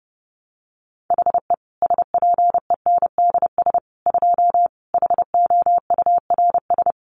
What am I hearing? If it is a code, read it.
Morse code, heard as a keyed tone, about 30 wpm, 700 Hz. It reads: HE SPENDS 2 HOURS
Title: Additional Easy Phrases 30wpm